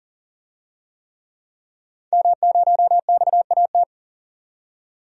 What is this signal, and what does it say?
Morse code, heard as a keyed tone, about 37 wpm, 700 Hz. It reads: M0XAT
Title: DXCC-100 Call Signs - 5 Characters - 40wpm